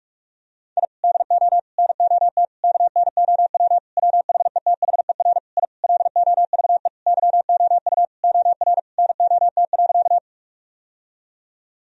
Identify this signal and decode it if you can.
Morse code, heard as a keyed tone, about 46 wpm, 700 Hz. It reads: I DO NOT KNOW WHETHER I LOVE YOU OR NOT.
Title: Sentences from Top 1000 Words - Review Part 3 - 45wpm